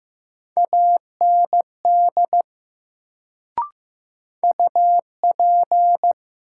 Morse code, heard as a keyed tone, about 15 wpm, 700 Hz. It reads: AND UP
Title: Sentences from Top 100 Words - New Word Drill - 15wpm